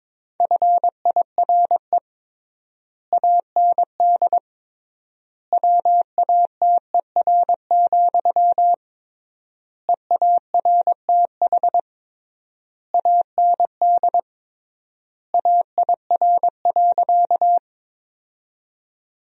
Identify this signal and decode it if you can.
Morse code, heard as a keyed tone, about 22 wpm, 700 Hz. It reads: FIRE AND WATER, EARTH AND AIR.
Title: 3x Word-Spacing - Sentences from Top 400 Words - 22wpm